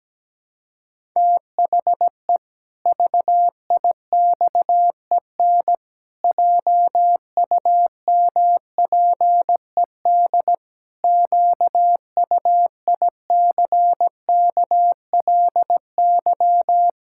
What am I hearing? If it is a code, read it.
Morse code, heard as a keyed tone, about 17 wpm, 700 Hz. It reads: THE VIXEN JUMPED QUICKLY
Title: Pangram Challenge v2 - 17wpm